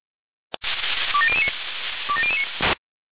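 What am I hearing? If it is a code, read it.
Five-tone selective call (ZVEI), heard as quick runs of five tones, tones of about 70 ms. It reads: 27009 27009
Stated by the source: ZVEI five-tone selective call received on an SDR.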